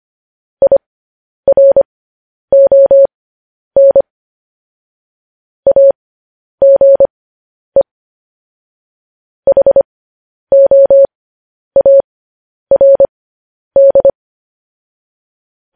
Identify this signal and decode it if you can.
Morse code, heard as a keyed tone, about 25 wpm, spaced as for 10 wpm, 550 Hz. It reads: IRON AGE HOARD
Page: news-headlines-in-morse-code-at-10-wpm